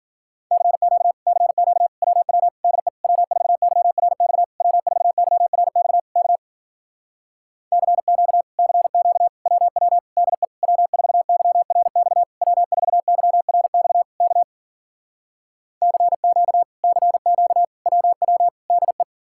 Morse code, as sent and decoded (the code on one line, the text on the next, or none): -.-. --.- / -.-. --.- / .-- .-- / -.. / .-- ....- -.-- .-. -..- / .-- ....- -.-- .-. -..- / -.- / -.-. --.- / -.-. --.- / .-- .-- / -.. / .-- ....- -.-- .-. -..- / .-- ....- -.-- .-. -..- / -.- / -.-. --.- / -.-. --.- / .-- .-- / -.. .
CQ CQ WW DE W4YRX W4YRX K CQ CQ WW DE W4YRX W4YRX K CQ CQ WW DE